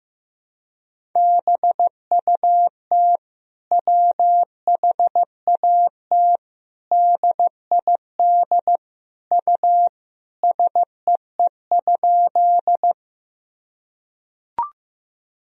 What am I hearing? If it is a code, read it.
Morse code, heard as a keyed tone, about 15 wpm, 700 Hz. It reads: BUT WHAT DID U SEE?
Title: Sentences from Top 100 Words and Abbreviations 15wpm